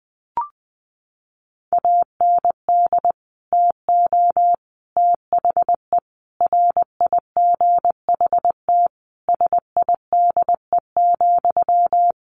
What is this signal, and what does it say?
Morse code, heard as a keyed tone, about 20 wpm, 700 Hz. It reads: AND TO THE RIGHT SIDE,
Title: Sentences from Top 1300 Words 20wpm